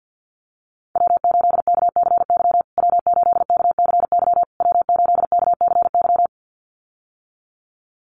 Morse code, heard as a keyed tone, about 50 wpm, 700 Hz. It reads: W8KCY W8KCY W8KCY
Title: USA Call Signs Repeated 3x 50wpm